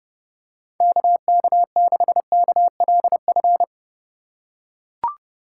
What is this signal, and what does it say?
Morse code, heard as a keyed tone, about 30 wpm, 700 Hz. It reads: KK6KLF
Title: USA Call Signs - 6 Characters - 4x Character Spacing - 30wpm